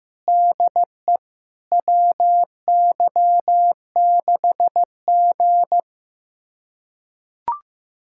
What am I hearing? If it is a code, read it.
Morse code, heard as a keyed tone, about 15 wpm, 700 Hz. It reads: DE WY6G
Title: USA Callsign DE Callsign - 4 Characters - 15wpm